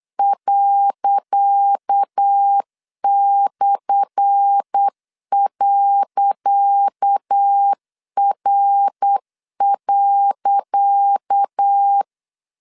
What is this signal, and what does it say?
Morse code, heard as a keyed tone, about 8 wpm, 800 Hz. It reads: ./.R.
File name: LEZIONI / 10 SIGNS / 10C DOT 8wpm 3649